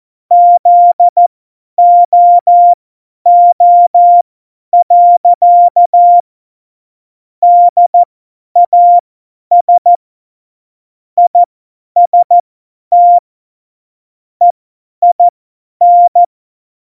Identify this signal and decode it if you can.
Morse code, heard as a keyed tone, about 14 wpm, spaced as for 10 wpm, 700 Hz. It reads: ZOO. DAS IST EIN
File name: Morsekurs / Witze / Witz 19. September 17, Tempo 50